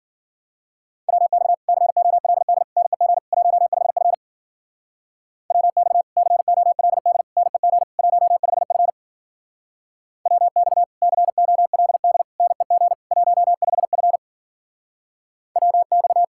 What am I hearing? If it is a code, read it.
Morse code, heard as a keyed tone, about 50 wpm, 700 Hz. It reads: WX COLD NEG 15F WX COLD NEG 15F WX COLD NEG 15F WX